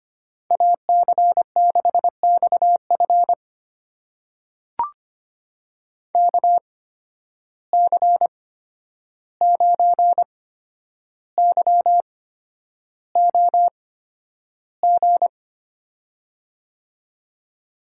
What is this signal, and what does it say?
Morse code, heard as a keyed tone, about 25 wpm, spaced as for 10 wpm, 700 Hz. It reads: AC6XF KC9YOG
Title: USA Call Signs - 8x Character Spacing - 25wpm